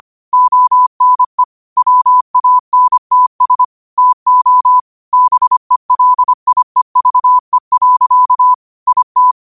Morse code, heard as a keyed tone, about 25 wpm, 1000 Hz. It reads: ONE WANTS TO BELIEVE. IT
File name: qotdfiles / 25wpm